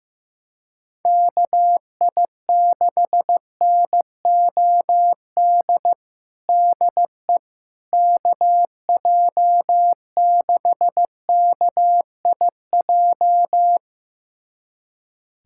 Morse code, heard as a keyed tone, about 15 wpm, 700 Hz. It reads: KI6NOD DE KJ6KIJ
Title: USA Callsign DE Callsign - 6 Characters - 15wpm